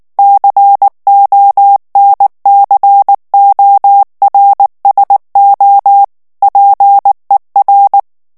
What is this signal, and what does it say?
Morse code, heard as a keyed tone, about 19 wpm, 800 Hz. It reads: CONCORSO PER
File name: morse mp3 / ITALIEN texte clair 2 vitesse 18 mots minute